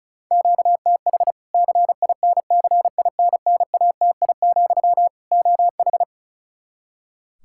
Morse code, heard as a keyed tone, about 36 wpm, 700 Hz. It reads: QTH CINCINNATI, OH
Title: Practice QSO - Exchange 1 - Beginning - 35wpm